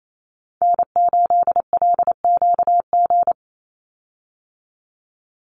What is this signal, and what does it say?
Morse code, heard as a keyed tone, about 28 wpm, 700 Hz. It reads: N8LQG
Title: USA Call Signs 28wpm